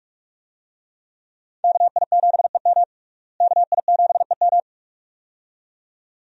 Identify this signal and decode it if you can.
Morse code, heard as a keyed tone, about 41 wpm, 700 Hz. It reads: KI7EM KI7EM
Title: USA Call Signs - 5 Characters - Repeated 2x - 45wpm